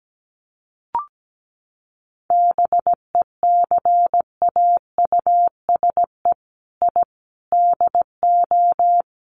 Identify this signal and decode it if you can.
Morse code, heard as a keyed tone, about 17 wpm, 700 Hz. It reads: BECAUSE I DO
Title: Sets of 4 Words 17wpm